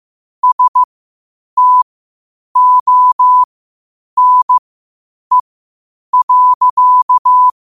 Morse code, heard as a keyed tone, about 15 wpm, spaced as for 8 wpm, 1000 Hz. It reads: STONE.
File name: qotdfiles / 8wpm